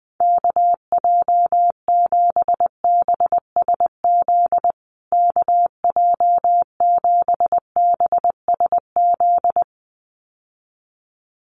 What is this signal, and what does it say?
Morse code, heard as a keyed tone, about 20 wpm, 700 Hz. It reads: KJ7BSZ KJ7BSZ
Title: USA Call Signs Repeated 2x 20wpm